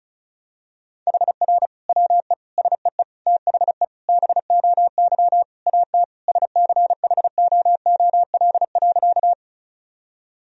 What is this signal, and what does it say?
Morse code, heard as a keyed tone, about 34 wpm, 700 Hz. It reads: HR WE SEE THE BOY AT SCHOOL.
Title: Sentences from Top 300 Words and Abbreviations 35wpm